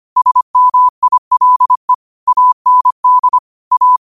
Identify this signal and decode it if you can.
Morse code, heard as a keyed tone, about 25 wpm, 1000 Hz. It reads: SMILE AND A
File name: qotdfiles / 25wpm